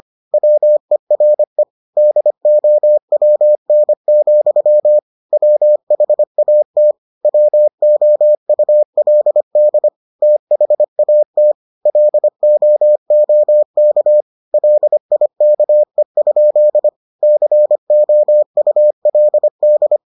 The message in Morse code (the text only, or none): WERE DOWN, WHAT WOULD THAT LOOK LIKE? COULD